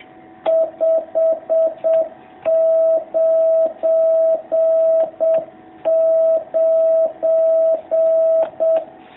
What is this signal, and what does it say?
Morse code, heard as a keyed tone, about 7 wpm, 660 Hz. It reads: 599
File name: jh8cbh / 2019cw